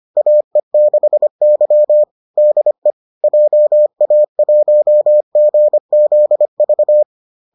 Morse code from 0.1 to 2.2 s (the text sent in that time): AE6Y